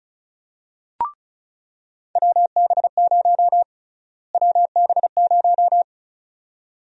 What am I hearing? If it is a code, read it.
Morse code, heard as a keyed tone, about 36 wpm, 700 Hz. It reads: WB0 WB0
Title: USA Call Sign Prefixes - Repeated 2x - 35wpm